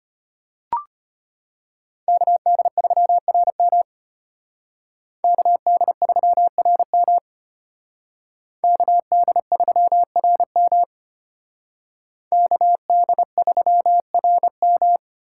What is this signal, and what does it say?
Morse code, heard as a keyed tone, about 32 wpm, 700 Hz. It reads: KD3RM KD3RM KD3RM KD3RM